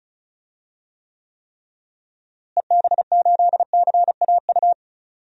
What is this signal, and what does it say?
Morse code, heard as a keyed tone, about 34 wpm, 700 Hz. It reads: EB8CAU
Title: DXCC-100 Call Signs - 6 Characters - Repeated 3x - 35wpm